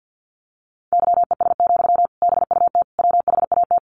Morse code, heard as a keyed tone, about 50 wpm, 700 Hz. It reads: YES, BUT WHAT
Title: Sentences from Top 500 Words 50wpm